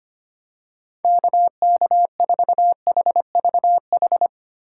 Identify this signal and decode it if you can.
Morse code, heard as a keyed tone, about 25 wpm, 700 Hz. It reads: KK4HVH
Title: USA Call Signs - 6 Characters - 4x Character Spacing - 25wpm